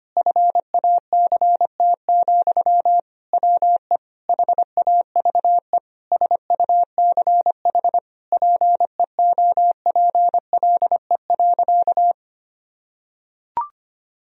Morse code, heard as a keyed tone, about 25 wpm, 700 Hz. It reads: FACT, WE HAVE SUCH PEOPLE.
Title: Sentences from Top 1000 Words - Review Part 3 - 25wpm